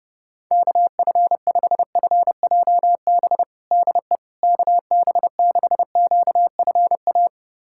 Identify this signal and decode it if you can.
Morse code, heard as a keyed tone, about 31 wpm, 700 Hz. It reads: KF5FJB DE KB6QFA
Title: USA Callsign DE Callsign - 6 Characters - 30wpm